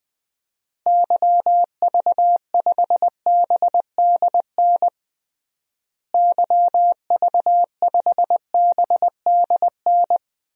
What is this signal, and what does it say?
Morse code, heard as a keyed tone, about 20 wpm, 700 Hz. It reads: YV5BDN YV5BDN
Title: DXCC-100 Call Signs - 6 Characters - Repeated 2x - 20wpm